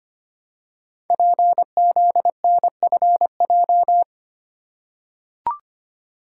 Morse code, heard as a keyed tone, about 25 wpm, 700 Hz. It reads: PZNFJ
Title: Code Groups - 5 Letters - 25wpm